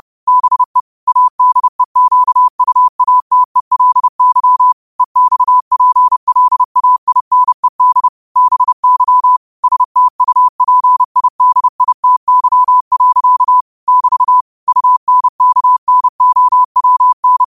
Morse code, heard as a keyed tone, about 30 wpm, 1000 Hz. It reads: BE ADEQUATELY EXPLAINED BY STUPIDITY. <BT> UNKNOWN
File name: qotdfiles / 30wpm